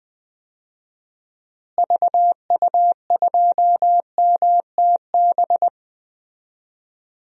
Morse code, heard as a keyed tone, about 21 wpm, 700 Hz. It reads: VU2MTB
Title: DXCC-100 Call Signs - 6 Characters - Repeated 3x - 20wpm